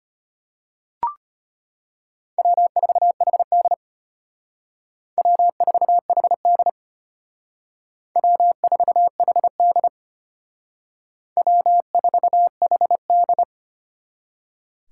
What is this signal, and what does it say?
Morse code, heard as a keyed tone, about 32 wpm, 700 Hz. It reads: W4HD W4HD W4HD W4HD